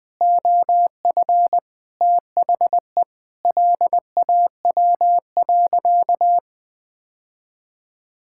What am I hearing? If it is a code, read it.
Morse code, heard as a keyed tone, about 20 wpm, 700 Hz. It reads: OF THE LAW.
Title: Sentences with 7-Word Phrases - 20wpm